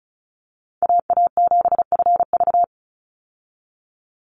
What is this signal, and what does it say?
Morse code, heard as a keyed tone, about 34 wpm, 700 Hz. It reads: AA7FV
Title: Call Signs Repeated 3x - 35wpm